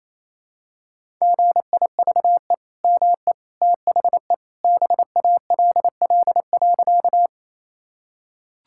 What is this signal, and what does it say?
Morse code, heard as a keyed tone, about 28 wpm, 700 Hz. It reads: GIVE ME THE BALL.
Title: Sentences from Top 1000 Words - Review Part 6 - 28wpm